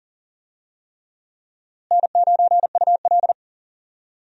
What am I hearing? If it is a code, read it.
Morse code, heard as a keyed tone, about 40 wpm, 700 Hz. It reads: N9UL
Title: USA Call Signs - 4 Characters - Repeated 3x - 40wpm